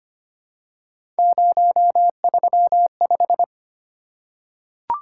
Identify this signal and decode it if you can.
Morse code, heard as a keyed tone, about 25 wpm, 700 Hz. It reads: 035
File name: Three Letter-Number Combinations - 25wpm